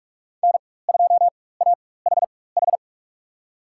Morse code, heard as a keyed tone, about 47 wpm, spaced as for 21 wpm, 700 Hz. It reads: N2AHH
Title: USA Call Signs - 5 Characters - 4x Character Spacing - 45wpm